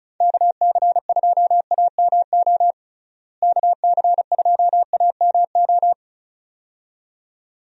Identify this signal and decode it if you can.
Morse code, heard as a keyed tone, about 35 wpm, 700 Hz. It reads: KC2AMO KC2AMO
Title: USA Call Signs - 6 Characters - Repeated 2x - 35wpm